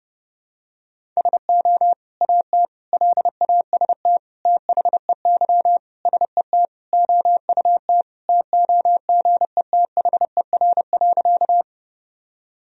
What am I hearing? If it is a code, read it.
Morse code, heard as a keyed tone, about 32 wpm, 700 Hz. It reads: SO AT LAST THEY SET OUT TOGETHER.
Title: Sentences from Top 1000 Words - Review Part 1 - 30wpm